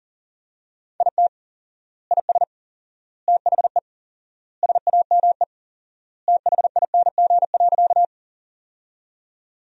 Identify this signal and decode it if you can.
Morse code, heard as a keyed tone, about 40 wpm, 700 Hz. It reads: IT IS THE SAME THING.
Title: 4x Word-Spacing - Sentences from Top 200 Words - 40wpm